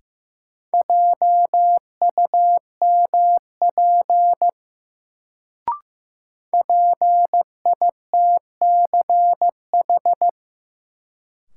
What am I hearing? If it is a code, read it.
Morse code, heard as a keyed tone, about 15 wpm, 700 Hz. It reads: JUMP PITCH
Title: Sentences from Top 1300 Words - New Word Drill Alt - 15wpm